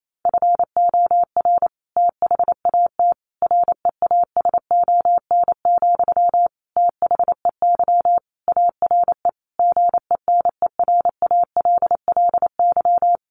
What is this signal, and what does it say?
Morse code, heard as a keyed tone, about 28 wpm, 700 Hz. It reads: FOR THAT REASON, THEY ARE GENERALLY
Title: Sentences from Top 1500 Words 28wpm